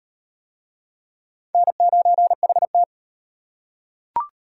Morse code, heard as a keyed tone, about 34 wpm, 700 Hz. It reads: N9HT